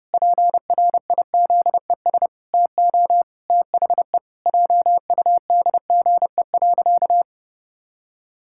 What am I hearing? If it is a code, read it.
Morse code, heard as a keyed tone, about 32 wpm, 700 Hz. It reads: PRIZES TO THE JUDGE.
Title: Pangram Challenge v2 - 30wpm